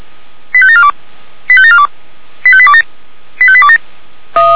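Five-tone selective call (ZVEI), heard as quick runs of five tones, tones of about 70 ms. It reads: 85742 85742 85827 85827
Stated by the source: received on SDR, ZVEI five-tone selective call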